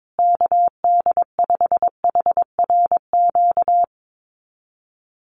Morse code, heard as a keyed tone, about 22 wpm, 700 Hz. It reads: KD5HRQ
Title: USA Call Signs Repeated 3x 22wpm